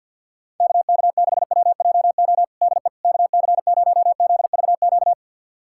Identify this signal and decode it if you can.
Morse code, heard as a keyed tone, about 50 wpm, 700 Hz. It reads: KK6WJO DE KC0ZVQ
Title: USA Callsign DE Callsign 50wpm